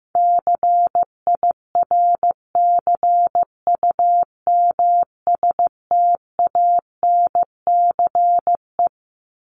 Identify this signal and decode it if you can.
Morse code, heard as a keyed tone, about 15 wpm, 700 Hz. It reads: CIRCUMSTANCE
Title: ES:S Suffix - Top 100 Words with Base Words 15wpm